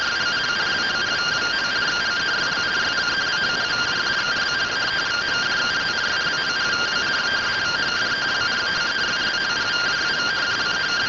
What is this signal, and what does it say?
Two-tone teleprinter signal, about 170 Hz shift.